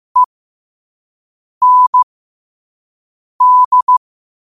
Morse code, heard as a keyed tone, about 15 wpm, spaced as for 5 wpm, 1000 Hz. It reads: END
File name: qotdfiles / 5wpm